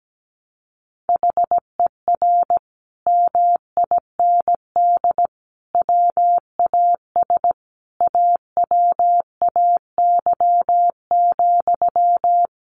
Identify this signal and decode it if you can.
Morse code, heard as a keyed tone, about 17 wpm, 700 Hz. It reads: HER MIND WAS AWAY,
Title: Sentences from Top 1500 Words 17wpm